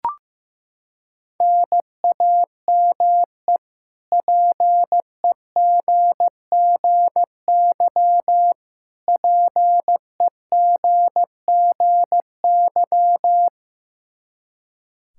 Morse code, heard as a keyed tone, about 15 wpm, 700 Hz. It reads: NAME PEGGY PEGGY
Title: K1USN SST QSO Practice 15wpm